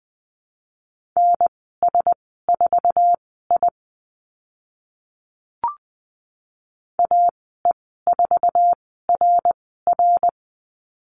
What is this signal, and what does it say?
Morse code, heard as a keyed tone, about 20 wpm, spaced as for 14 wpm, 700 Hz. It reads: NS4I AE4RR
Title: 062 Lesson 11g - Letter h - Call Signs 20wpm